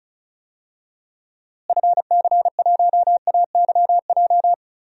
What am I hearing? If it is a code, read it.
Morse code, heard as a keyed tone, about 36 wpm, 700 Hz. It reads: FC1AYJ
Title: DXCC-100 Call Signs - 6 Characters - Repeated 3x - 35wpm